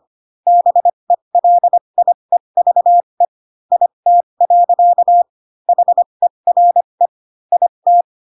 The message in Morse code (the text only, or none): BELIEVE IT. HERE IT